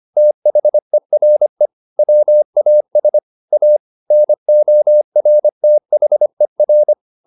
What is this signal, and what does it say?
Morse code, heard as a keyed tone, about 25 wpm, 600 Hz. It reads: THERE WAS A NORTHER